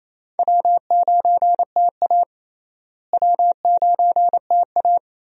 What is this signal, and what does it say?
Morse code, heard as a keyed tone, about 28 wpm, 700 Hz. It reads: W9TA W9TA
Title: USA Call Signs - 4 Characters - Repeated 3x - 28wpm